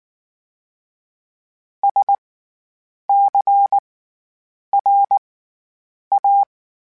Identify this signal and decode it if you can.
Morse code, heard as a keyed tone, about 19 wpm, spaced as for 8 wpm, 800 Hz. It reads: SCRA
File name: Special INGs / Special INGs-mp3s / SlowToFaster ing @18